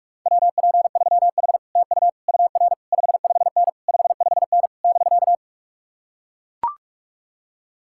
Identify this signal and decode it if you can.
Morse code, heard as a keyed tone, about 44 wpm, 700 Hz. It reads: WP3H TU UR 55N 55N <BK>